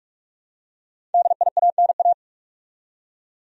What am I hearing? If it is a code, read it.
Morse code, heard as a keyed tone, about 46 wpm, 700 Hz. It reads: DIANA
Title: USA Common Names - 45wpm